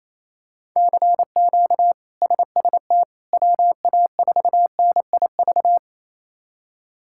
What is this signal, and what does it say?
Morse code, heard as a keyed tone, about 28 wpm, 700 Hz. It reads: CQ SST WA4NIV
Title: K1USN SST QSO Practice - 28wpm